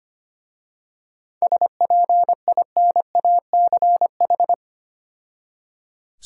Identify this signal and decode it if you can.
Morse code, heard as a keyed tone, about 25 wpm, 700 Hz. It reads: SPINACH
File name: practice / 25wpm / Difficult Words with Hints - 25wpm